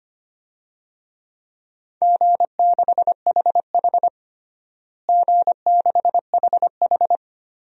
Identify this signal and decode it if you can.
Morse code, heard as a keyed tone, about 25 wpm, 700 Hz. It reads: G6HH G6HH
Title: DXCC-100 Call Signs - 4 Characters - Repeated 3x - 25wpm